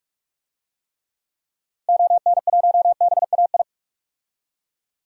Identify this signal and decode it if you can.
Morse code, heard as a keyed tone, about 43 wpm, 700 Hz. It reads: ON1BAI
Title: DXCC-100 Call Signs - 6 Characters - Repeated 2x - 45wpm